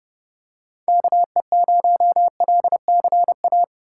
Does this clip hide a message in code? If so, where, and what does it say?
Yes: Morse code from 0.9 to 3.6 s, heard as a keyed tone, about 28 wpm, 700 Hz. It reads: KE0LCA